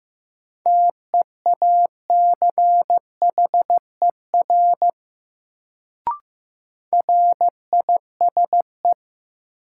Morse code, heard as a keyed tone, about 15 wpm, 700 Hz. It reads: TEACHER RISE
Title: Sentences from Top 900 Words - New Word Drill - 15wpm